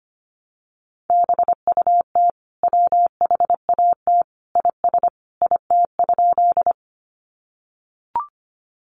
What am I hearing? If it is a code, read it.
Morse code, heard as a keyed tone, about 25 wpm, 700 Hz. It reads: BUT WHAT IS IT?
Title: Sentences from Top 100 Words 25wpm